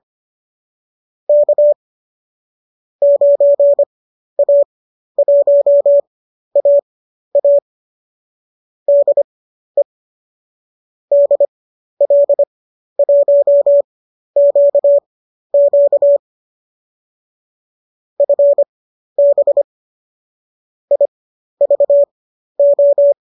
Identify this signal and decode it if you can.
Morse code, heard as a keyed tone, about 25 wpm, spaced as for 12 wpm, 570 Hz. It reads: K 9A1AA DE DL1QQ FB IVO
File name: qso012 12